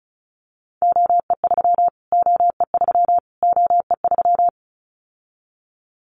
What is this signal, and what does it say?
Morse code, heard as a keyed tone, about 34 wpm, 700 Hz. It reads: OE3 OE3 OE3
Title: Callsign Prefixes for Top 25 DXCC - Repeated 3x - 35wpm